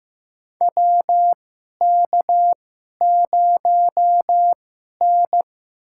Morse code, heard as a keyed tone, about 15 wpm, spaced as for 11 wpm, 700 Hz. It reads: WK0N
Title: USA Call Signs - 2x Character Spacing - 15wpm